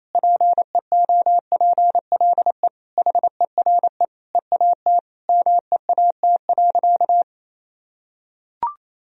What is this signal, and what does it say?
Morse code, heard as a keyed tone, about 28 wpm, 700 Hz. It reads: PEOPLE HERE EAT MEAT.
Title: Sentences from Top 1000 Words - Review Part 3 - 28wpm